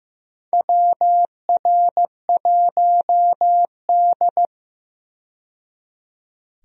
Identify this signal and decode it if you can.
Morse code, heard as a keyed tone, about 15 wpm, 700 Hz. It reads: WR1D
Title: USA Call Signs - 4 Characters - 15wpm